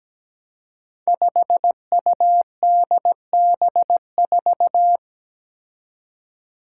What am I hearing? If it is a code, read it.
Morse code, heard as a keyed tone, about 17 wpm, 700 Hz. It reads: 5UDB4
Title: Five Challenging Letter-Number Combinations 17wpm